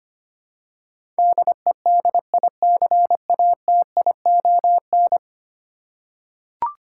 Morse code, heard as a keyed tone, about 25 wpm, 700 Hz. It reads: DEDICATION
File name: 10 Letter Words - First 3 Letters Given - 25wpm